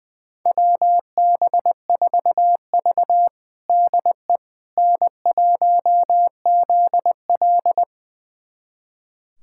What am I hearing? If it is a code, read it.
Morse code, heard as a keyed tone, about 20 wpm, 700 Hz. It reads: WB4V DE N1ZL
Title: USA Callsign DE Callsign - 4 Characters - 20wpm